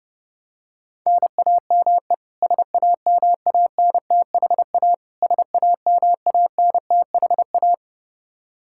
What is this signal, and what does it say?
Morse code, heard as a keyed tone, about 30 wpm, 700 Hz. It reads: NAME SAMANTHA SAMANTHA
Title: Practice QSO - Exchange 1 - Beginning - 30wpm